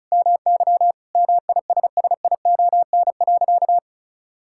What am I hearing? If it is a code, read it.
Morse code, heard as a keyed tone, about 36 wpm, 700 Hz. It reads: MY MISSION.
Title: Sentences from Top 1600 Words 35wpm